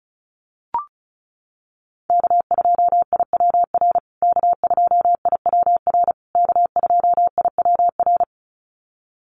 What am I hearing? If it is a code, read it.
Morse code, heard as a keyed tone, about 34 wpm, 700 Hz. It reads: K2IWR K2IWR K2IWR
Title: USA Call Signs Repeated 3x 35wpm